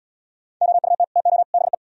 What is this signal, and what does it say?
Morse code, heard as a keyed tone, about 72 wpm, 700 Hz. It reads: OUT TO BE